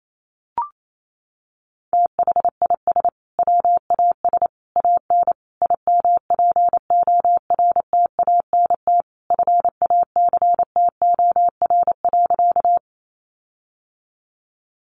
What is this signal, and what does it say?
Morse code, heard as a keyed tone, about 28 wpm, 700 Hz. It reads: THIS WAS AN IMPORTANT FACTOR.
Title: Sentences from Top 1300 Words 28wpm